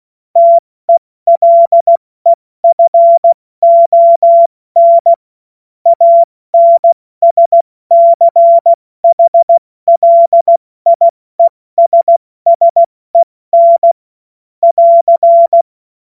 Morse code, heard as a keyed tone, about 16 wpm, 680 Hz. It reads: TELEFON ANSCHLIESSEN <AR>
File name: Morsekurs / Witze / Witz 06. Dezember 2022 70 F80